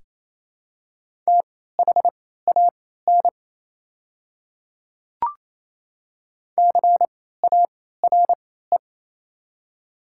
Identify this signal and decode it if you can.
Morse code, heard as a keyed tone, about 28 wpm, spaced as for 16 wpm, 700 Hz. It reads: THAN CARE
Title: 3x Character Spacing - Top 300 Words - 28wpm